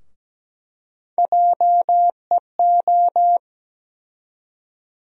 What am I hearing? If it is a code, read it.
Morse code, heard as a keyed tone, about 17 wpm, 700 Hz. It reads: JEO
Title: USA Call Sign Postfixes - 3 Letters 17wpm